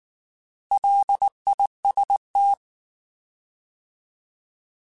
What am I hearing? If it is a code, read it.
Morse code, heard as a keyed tone, about 19 wpm, 800 Hz. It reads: LIST
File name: Alphabetical / Alphabetical mp3s / Ls @18